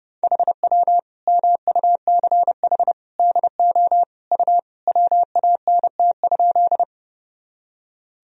Morse code, heard as a keyed tone, about 32 wpm, 700 Hz. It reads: HW MUCH DO U WANT?
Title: Sentences from Top 200 Words and Abbreviations 30wpm